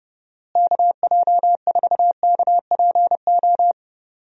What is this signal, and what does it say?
Morse code, heard as a keyed tone, about 30 wpm, 700 Hz. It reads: KJ4KPO
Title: USA Call Signs - 6 Characters - Repeated 2x - 30wpm